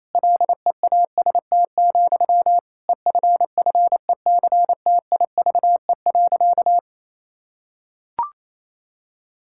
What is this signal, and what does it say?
Morse code, heard as a keyed tone, about 28 wpm, 700 Hz. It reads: LEAST, EFFECTIVE.
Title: Sentences from Top 1000 Words - Review Part 6 - 28wpm